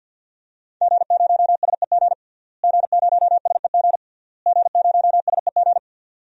Morse code, heard as a keyed tone, about 50 wpm, 700 Hz. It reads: G0SEG G0SEG G0SEG
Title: DXCC-100 Call Signs - 5 Characters - Repeated 3x - 50wpm